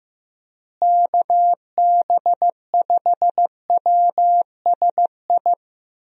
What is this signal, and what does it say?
Morse code, heard as a keyed tone, about 15 wpm, 700 Hz. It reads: KB5WSI
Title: USA Call Signs - 6 Characters - Repeated 3x - 15wpm